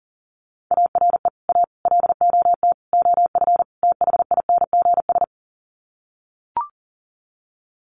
Morse code, heard as a keyed tone, about 40 wpm, 700 Hz. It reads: ARE A LOT OF THINGS
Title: Sets of 5 Words Encore2 40wpm